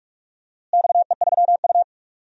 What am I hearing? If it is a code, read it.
Morse code, heard as a keyed tone, about 46 wpm, 700 Hz. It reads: XE3U
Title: DXCC-100 Call Signs - 4 Characters - Repeated 2x - 45wpm